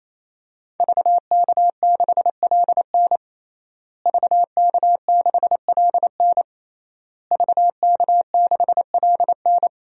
Morse code, heard as a keyed tone, about 28 wpm, 700 Hz. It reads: VK6LN VK6LN VK6LN
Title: DXCC-100 Call Signs - 5 Characters - Repeated 3x - 28wpm